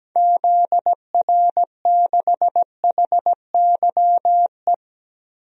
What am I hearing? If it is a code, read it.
Morse code, heard as a keyed tone, about 17 wpm, 700 Hz. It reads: ZR6HYE
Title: DXCC-100 Call Signs - 6 Characters - Repeated 2x - 17wpm